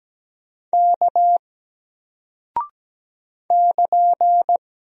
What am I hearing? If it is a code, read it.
Morse code, heard as a keyed tone, about 17 wpm, 700 Hz. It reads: K <KN>
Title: Single Character (Letter:Number:Punctuation:Prosign) Alt 17wpm